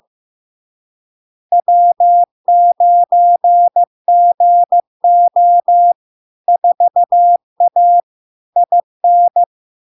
Morse code, heard as a keyed tone, about 15 wpm, 700 Hz. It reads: W9GO 4A IN
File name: LARK-FD-2022-Log-in-CW-15wpm-700-Hz